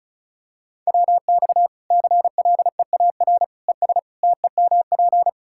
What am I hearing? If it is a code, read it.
Morse code, heard as a keyed tone, about 34 wpm, 700 Hz. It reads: WX CLEAR ES TEMP
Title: K1USN SST QSO Practice 35wpm